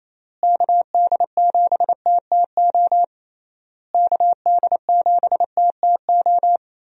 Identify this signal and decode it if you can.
Morse code, heard as a keyed tone, about 28 wpm, 700 Hz. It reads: KD7TTO KD7TTO
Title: USA Call Signs - 6 Characters - Repeated 3x - 28wpm